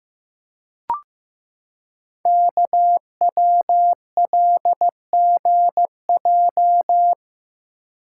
Morse code, heard as a keyed tone, about 15 wpm, 700 Hz. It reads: KWLGJ
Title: Code Groups - 5 Letters - 15wpm